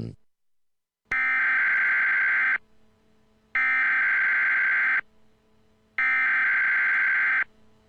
Emergency Alert System (SAME) tones (header bursts), heard as harsh buzzing bursts. It is heard from 1.1 to 7.4 s.
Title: Dead Reign EAS From President Kelly